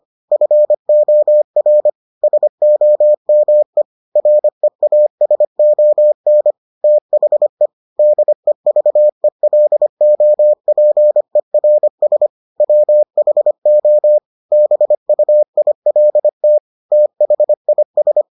Morse code, heard as a keyed tone, about 25 wpm, 600 Hz. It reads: FOR SOME REASON THE DEVELOPERS WHO BUILT THIS